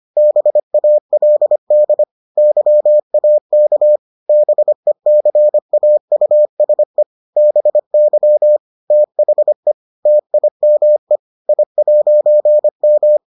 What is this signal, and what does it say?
Morse code, heard as a keyed tone, about 25 wpm, 600 Hz. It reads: BALD YAK BECAUSE BY THE TIME I'M